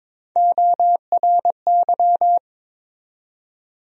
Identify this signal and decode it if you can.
Morse code, heard as a keyed tone, about 22 wpm, 700 Hz. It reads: ORY
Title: USA Call Sign Postfixes - 3 Letters 22wpm